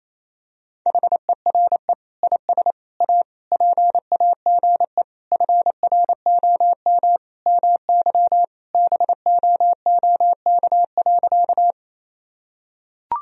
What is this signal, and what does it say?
Morse code, heard as a keyed tone, about 28 wpm, 700 Hz. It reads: HERE IS A PAGE FROM MY BOOK.
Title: Sentences from Top 1000 Words - Review Part 1 - 28wpm